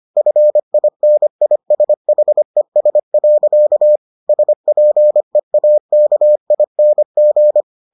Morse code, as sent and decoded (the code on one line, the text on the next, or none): ..-. .. -. .. ... .... ... .-.-.- / ... .--. .- -.- .. -. --.
FINISHES. SPEAKING